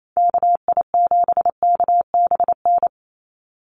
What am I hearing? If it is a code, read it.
Morse code, heard as a keyed tone, about 28 wpm, 700 Hz. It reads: KI7KBN
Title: USA Call Signs 28wpm